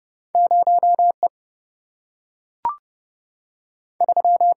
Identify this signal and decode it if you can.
Morse code, heard as a keyed tone, about 30 wpm, 700 Hz. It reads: 0E 3
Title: Single-Letter Single-Number 30wpm